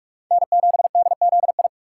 Morse code, heard as a keyed tone, about 44 wpm, 700 Hz. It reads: N7DZI
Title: USA Call Signs - 5 Characters - 4x Character Spacing - 45wpm